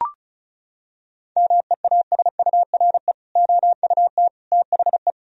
Morse code, heard as a keyed tone, about 34 wpm, 700 Hz. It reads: MEASURE OUT THE
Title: Sentences from Top 1000 Words - Review Part 2 - 35wpm